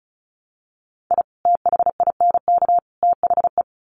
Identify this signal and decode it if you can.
Morse code, heard as a keyed tone, about 34 wpm, 700 Hz. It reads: I THINK THE
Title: Sets of 3 Words 35wpm